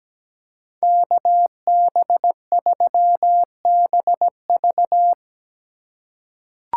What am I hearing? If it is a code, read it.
Morse code, heard as a keyed tone, about 17 wpm, 700 Hz. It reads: KB3BV
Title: USA Call Signs - 5 Characters - 2x Character Spacing - 17wpm